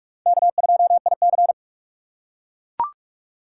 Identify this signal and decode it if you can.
Morse code, heard as a keyed tone, about 44 wpm, 700 Hz. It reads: K2IC